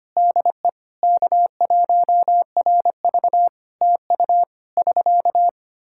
Morse code, heard as a keyed tone, about 25 wpm, 700 Hz. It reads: DE K1RV TU <SK>
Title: K1USN SST QSO Practice 25wpm